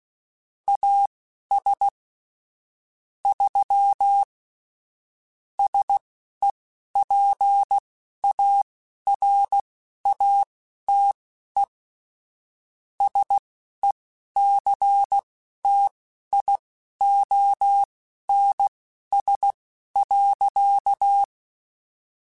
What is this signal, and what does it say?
Morse code, heard as a keyed tone, about 16 wpm, spaced as for 11 wpm, 800 Hz. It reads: AS 3 SEPARATE SECTIONS.
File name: Technical Subjects / Technical Subjects-mp3s / Sw Pwr Sup @10